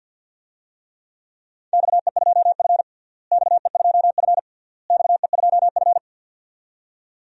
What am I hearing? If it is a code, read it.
Morse code, heard as a keyed tone, about 52 wpm, 700 Hz. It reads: XE2F XE2F XE2F
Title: DXCC-100 Call Signs - 4 Characters - Repeated 3x - 50wpm